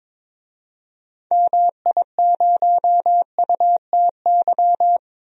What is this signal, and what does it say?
Morse code, heard as a keyed tone, about 22 wpm, 700 Hz. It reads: MI0UTY
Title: DXCC-100 Call Signs - 6 Characters - 22wpm